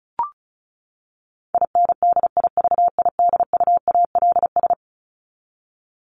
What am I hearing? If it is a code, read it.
Morse code, heard as a keyed tone, about 34 wpm, 700 Hz. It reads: INDIVIDUALS
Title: ES:S Suffix - Top 100 Words with Base Words 35wpm